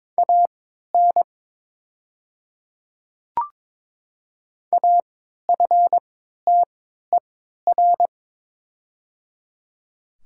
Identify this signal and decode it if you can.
Morse code, heard as a keyed tone, about 22 wpm, spaced as for 13 wpm, 700 Hz. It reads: AN AFTER
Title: 3x Character Spacing - Top 300 Words - 22wpm